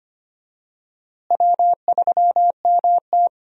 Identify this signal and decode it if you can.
Morse code, heard as a keyed tone, about 25 wpm, 700 Hz. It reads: W3MT
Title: USA Call Signs - 4 Characters - Repeated 3x - 25wpm